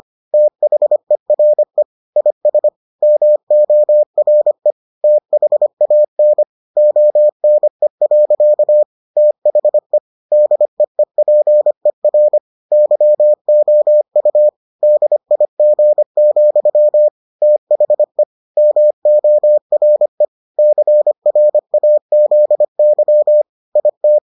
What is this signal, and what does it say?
Morse code, heard as a keyed tone, about 25 wpm, 600 Hz. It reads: THERE IS MORE THAN ONE. THE DEEPER YOU DIG, THE MORE CRAZY IT